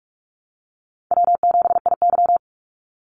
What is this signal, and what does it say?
Morse code, heard as a keyed tone, about 44 wpm, 700 Hz. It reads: W7IY
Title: Call Signs Repeated 2x - 45wpm